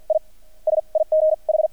Morse code, heard as a keyed tone, about 48 wpm, spaced as for 26 wpm, 630 Hz. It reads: IS IGH
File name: archived / CRSnet